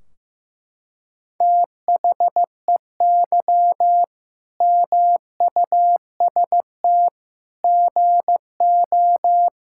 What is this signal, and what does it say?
Morse code, heard as a keyed tone, about 16 wpm, 700 Hz. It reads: THEY MUST GO
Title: Sentences from Top 1600 Words 15wpm